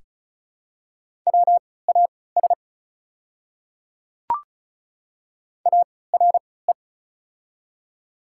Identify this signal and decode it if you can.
Morse code, heard as a keyed tone, about 36 wpm, spaced as for 20 wpm, 700 Hz. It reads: WAS ARE
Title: 3x Character Spacing - Top 200 Words - 35wpm